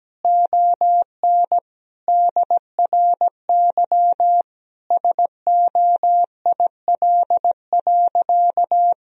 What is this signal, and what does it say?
Morse code, heard as a keyed tone, about 17 wpm, 700 Hz. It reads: ON DRY SOIL.
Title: Sentences from Top 1000 Words - Review Part 4 - 17wpm